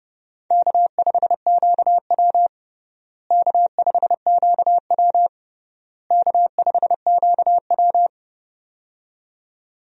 Morse code, heard as a keyed tone, about 30 wpm, 700 Hz. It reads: K5QW K5QW K5QW
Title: USA Call Signs - 4 Characters - Repeated 3x - 30wpm